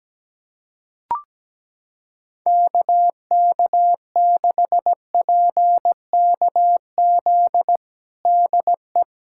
Morse code, heard as a keyed tone, about 17 wpm, 700 Hz. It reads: KK6PKZ DE
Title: USA Callsign DE Callsign 17wpm